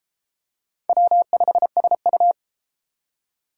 Morse code, heard as a keyed tone, about 32 wpm, 700 Hz. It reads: W5SU